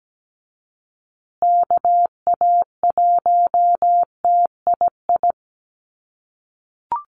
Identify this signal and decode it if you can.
Morse code, heard as a keyed tone, about 17 wpm, 700 Hz. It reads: KA1TII
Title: USA Call Signs Repeated 3x 17wpm